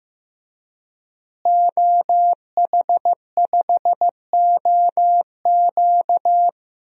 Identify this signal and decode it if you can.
Morse code, heard as a keyed tone, about 16 wpm, 700 Hz. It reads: OH5OQ
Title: DXCC-100 Call Signs - 5 Characters - Repeated 3x - 15wpm